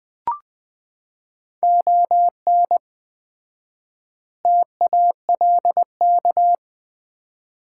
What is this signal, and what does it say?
Morse code, heard as a keyed tone, about 20 wpm, 700 Hz. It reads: ON TALK
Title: 4x Word-Spacing - Sets of 5 Words - 20wpm